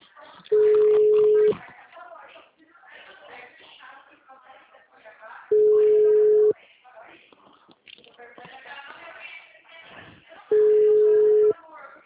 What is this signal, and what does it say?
Ringback tone (Europe, 425 Hz), with speech.
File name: out- -2010- - - .83756